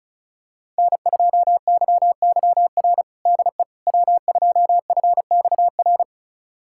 Morse code, heard as a keyed tone, about 36 wpm, 700 Hz. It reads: N2YYR DE W2FXR
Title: USA Callsign DE Callsign - 5 Characters - 35wpm